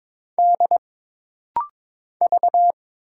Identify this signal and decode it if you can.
Morse code, heard as a keyed tone, about 22 wpm, 700 Hz. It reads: D V
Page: This rapid-fire timing is geared to push your ICR (Instant Character Recognition) skills.